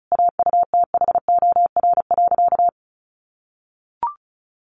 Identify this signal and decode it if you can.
Morse code, heard as a keyed tone, about 34 wpm, 700 Hz. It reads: AUTHOR.
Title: Sentences from Top 1200 Words 35wpm